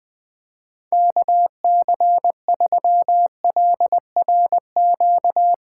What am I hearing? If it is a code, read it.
Morse code, heard as a keyed tone, about 19 wpm, 700 Hz. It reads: KC3LRQ
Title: Speed Racing - USA 2x3 Call Signs 17wpm